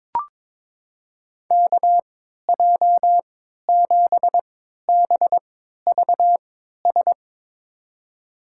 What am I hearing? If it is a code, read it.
Morse code, heard as a keyed tone, about 22 wpm, spaced as for 13 wpm, 700 Hz. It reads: KJ7BVS
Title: USA Call Signs - 6 Characters - 3x Character Spacing - 22wpm